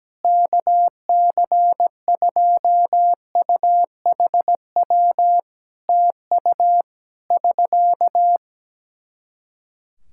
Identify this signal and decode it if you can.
Morse code, heard as a keyed tone, about 17 wpm, 700 Hz. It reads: KC2UHW TU <SK>
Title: K1USN SST QSO Practice 17wpm